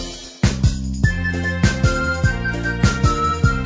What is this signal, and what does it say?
There is no code.